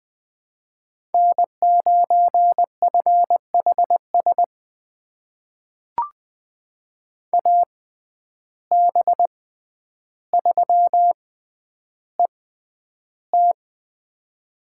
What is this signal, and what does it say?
Morse code, heard as a keyed tone, about 20 wpm, spaced as for 10 wpm, 700 Hz. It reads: N9FHS AB3ET
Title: USA Call Signs - 5 Characters - 6x Character Spacing - 20wpm